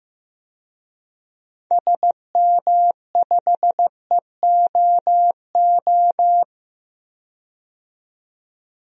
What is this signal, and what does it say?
Morse code, heard as a keyed tone, about 15 wpm, 700 Hz. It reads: SM5EOO
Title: DXCC-100 Call Signs - 6 Characters - 15wpm